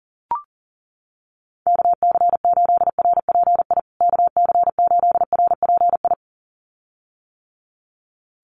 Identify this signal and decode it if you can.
Morse code, heard as a keyed tone, about 40 wpm, 700 Hz. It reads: KC8RPI KC8RPI
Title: USA Call Signs Repeated 2x 40wpm